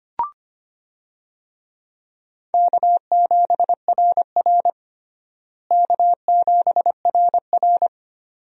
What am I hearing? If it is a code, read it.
Morse code, heard as a keyed tone, about 25 wpm, 700 Hz. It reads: K7RR K7RR
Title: USA Call Signs - 4 Characters - Repeated 2x - 25wpm